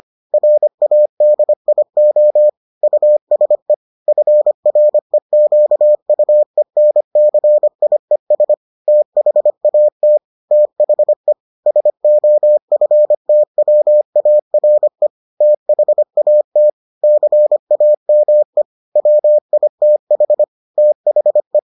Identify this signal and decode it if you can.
Morse code, heard as a keyed tone, about 25 wpm, 600 Hz. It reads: RADIO USE FREQUENCIES THAT THE SOFTWARE THAT CAME WITH THE